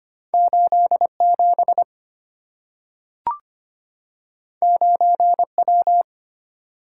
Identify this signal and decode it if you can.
Morse code, heard as a keyed tone, about 25 wpm, 700 Hz. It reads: 8 7 9 W
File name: Two Letter-Number Combinations - 25wpm